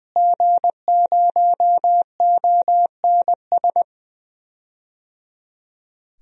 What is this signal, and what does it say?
Morse code, heard as a keyed tone, about 20 wpm, 700 Hz. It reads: G0ONS
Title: DXCC-100 Call Signs - 5 Characters - Repeated 3x - 20wpm